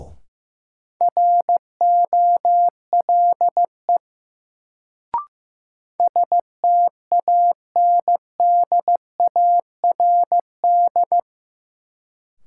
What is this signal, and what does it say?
Morse code, heard as a keyed tone, about 15 wpm, 700 Hz. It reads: ROLE STANDARD
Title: Sentences from Top 1000 Words - Review Part 6 - New Word Drill - 15wpm